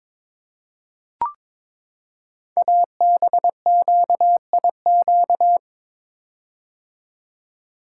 Morse code, heard as a keyed tone, about 23 wpm, 700 Hz. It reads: ABQIQ
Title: Code Groups - 5 Letters - 22wpm